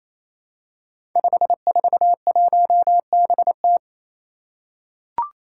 Morse code, heard as a keyed tone, about 27 wpm, 700 Hz. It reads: 541BT
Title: Five Challenging Letter-Number Combinations 28wpm